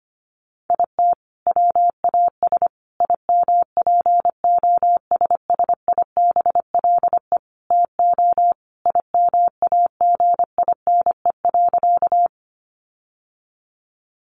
Morse code, heard as a keyed tone, about 25 wpm, 700 Hz. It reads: IT WAS IMPOSSIBLE TO IMAGINE.
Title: Sentences from Top 1400 Words 25wpm